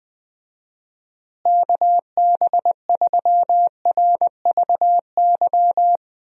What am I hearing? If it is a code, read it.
Morse code, heard as a keyed tone, about 20 wpm, 700 Hz. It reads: KB3RVY
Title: USA Call Signs - 6 Characters - Repeated 3x - 20wpm